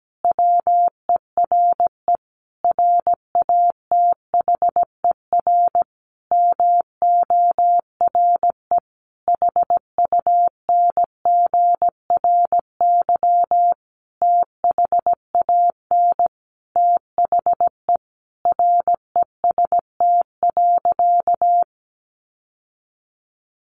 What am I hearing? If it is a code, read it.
Morse code, heard as a keyed tone, about 17 wpm, 700 Hz. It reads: WERE RATHER MORE HUNGRY THAN THE REST.
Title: Sentences from Top 1400 Words 17wpm